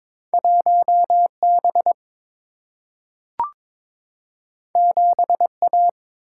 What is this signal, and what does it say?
Morse code, heard as a keyed tone, about 22 wpm, 700 Hz. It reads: 1 B 7 A
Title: Single-Letter Single-Number 22wpm